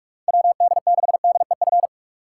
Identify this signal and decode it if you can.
Morse code, heard as a keyed tone, about 46 wpm, 700 Hz. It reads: WD6DEF
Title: USA Call Signs - 2x Character Spacing - 45wpm